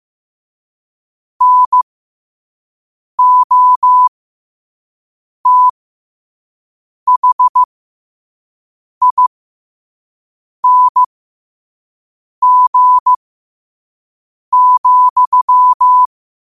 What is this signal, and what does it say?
Morse code, heard as a keyed tone, about 15 wpm, spaced as for 5 wpm, 1000 Hz. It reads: NOTHING,
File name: qotdfiles / 5wpm